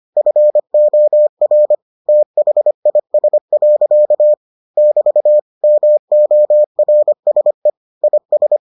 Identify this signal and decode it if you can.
Morse code, heard as a keyed tone, about 25 wpm, 600 Hz. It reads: FOR THIS. <BT> MORSE IS